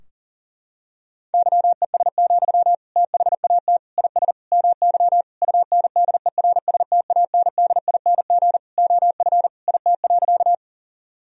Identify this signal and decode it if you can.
Morse code, heard as a keyed tone, about 37 wpm, 700 Hz. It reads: YES, THAT IS MY UNDERSTANDING OF IT.
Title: Sentences from Top 1000 Words - Review Part 3 - 40wpm